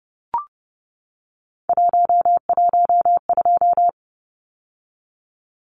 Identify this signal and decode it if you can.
Morse code, heard as a keyed tone, about 30 wpm, 700 Hz. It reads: 112
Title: Numbers - 3-Digit 30wpm